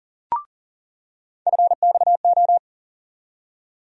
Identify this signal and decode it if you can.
Morse code, heard as a keyed tone, about 40 wpm, 700 Hz. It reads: FXO
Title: USA Call Sign Postfixes - 3 Letters 40wpm